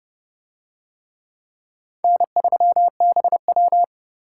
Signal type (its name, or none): Morse code